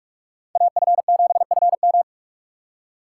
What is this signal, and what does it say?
Morse code, heard as a keyed tone, about 46 wpm, 700 Hz. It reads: AF7FM
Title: USA Call Signs - 3x Character Spacing - 45wpm